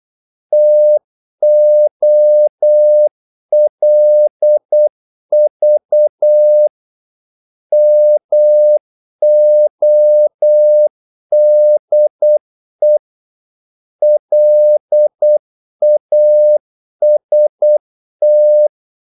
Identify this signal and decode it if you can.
Morse code, heard as a keyed tone, about 8 wpm, 600 Hz. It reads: TOLV MODE LAST
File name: Korte ord DK 8wpm